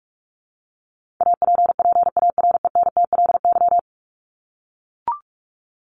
Morse code, heard as a keyed tone, about 44 wpm, 700 Hz. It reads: APPARENTLY
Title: LY Suffix - Top 100 Words with Base Words 45wpm